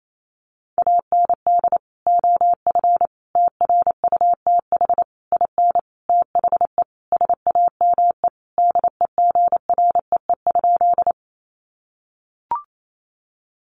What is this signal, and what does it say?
Morse code, heard as a keyed tone, about 28 wpm, 700 Hz. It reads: AND OF TRUTH IN THE SAME DEGREE?
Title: Sentences from Top 1500 Words 28wpm